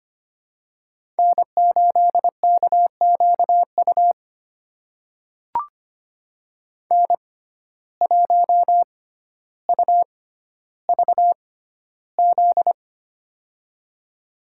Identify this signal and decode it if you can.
Morse code, heard as a keyed tone, about 25 wpm, spaced as for 13 wpm, 700 Hz. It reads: N8KQU N1UVZ
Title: USA Call Signs - 5 Characters - 6x Character Spacing - 25wpm